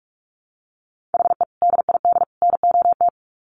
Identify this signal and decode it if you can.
Morse code, heard as a keyed tone, about 44 wpm, 700 Hz. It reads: HE DID NOT
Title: Sets of 3 Words 45wpm